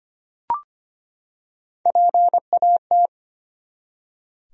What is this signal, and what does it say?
Morse code, heard as a keyed tone, about 25 wpm, 700 Hz. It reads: PAT
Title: USA Common Names - Max 5 Letters - 25wpm